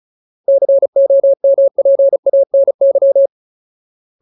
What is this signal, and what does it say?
Morse code, heard as a keyed tone, about 36 wpm, 550 Hz. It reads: COMPANY
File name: Ultimate Word List / 35WPM Ultimate